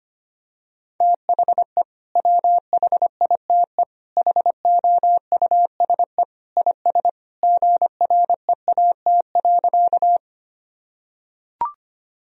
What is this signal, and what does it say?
Morse code, heard as a keyed tone, about 25 wpm, 700 Hz. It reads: THE WHITE HOUSE IS GREAT.
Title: Sentences from Top 1000 Words - Review Part 1 - 25wpm